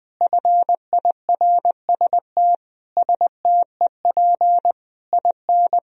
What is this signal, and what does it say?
Morse code, heard as a keyed tone, about 21 wpm, 700 Hz. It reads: FIRST STEP IN
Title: Sentences from Top 1000 Words - Review Part 4 - 20wpm